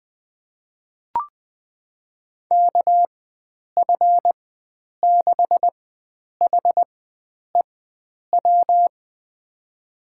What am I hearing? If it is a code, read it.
Morse code, heard as a keyed tone, about 21 wpm, spaced as for 9 wpm, 700 Hz. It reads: KF6HEW
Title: USA Call Signs - 6 Characters - 4x Character Spacing - 20wpm